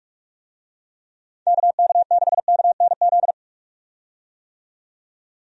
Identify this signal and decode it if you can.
Morse code, heard as a keyed tone, about 43 wpm, 700 Hz. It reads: KK6KNZ
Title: USA Call Signs - 6 Characters - 45wpm